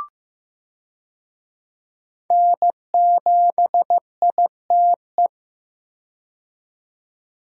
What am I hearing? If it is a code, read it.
Morse code, heard as a keyed tone, about 15 wpm, 700 Hz. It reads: N7ITE
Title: USA Call Signs - 5 Characters - 15wpm